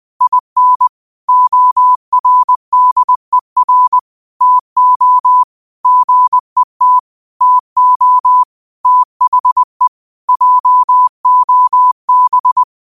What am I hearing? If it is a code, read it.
Morse code, heard as a keyed tone, about 20 wpm, 1000 Hz. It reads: IN ORDER TO GET TO THE JOB
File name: 20wpm